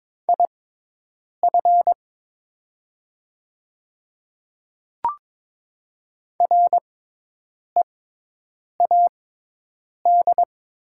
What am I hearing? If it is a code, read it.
Morse code, heard as a keyed tone, about 22 wpm, spaced as for 8 wpm, 700 Hz. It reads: IF READ
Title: 6x Character Spacing - Top 500 Words - 22wpm